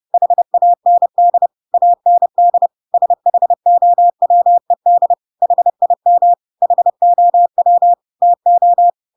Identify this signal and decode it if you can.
Morse code, heard as a keyed tone, about 30 wpm, 700 Hz. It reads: HAND AND SHOWED HIM HOW TO
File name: Pinocchio / Pinocchio ch3 30 30 ik1hge